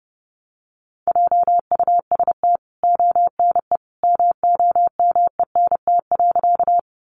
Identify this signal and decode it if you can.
Morse code, heard as a keyed tone, about 30 wpm, 700 Hz. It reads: JUST ONE MOMENT.
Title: Sentences from Top 1000 Words 30wpm